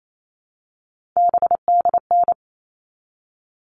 Morse code, heard as a keyed tone, about 28 wpm, 700 Hz. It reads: BDN
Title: All Cut Numbers - 3-Digit 28wpm